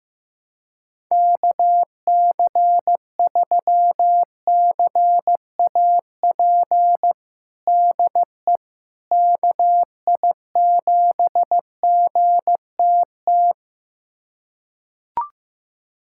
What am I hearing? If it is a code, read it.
Morse code, heard as a keyed tone, about 15 wpm, 700 Hz. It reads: KC3CAP DE KI7GTT
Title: USA Callsign DE Callsign 15wpm